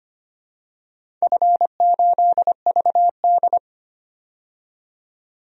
Morse code, heard as a keyed tone, about 25 wpm, 700 Hz. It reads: F8VD
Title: DXCC-100 Call Signs - 4 Characters - 25wpm